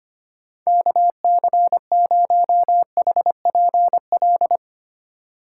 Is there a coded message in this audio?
Yes: Morse code from 0.7 to 4.6 s, heard as a keyed tone, about 33 wpm, 700 Hz. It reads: KC0HPL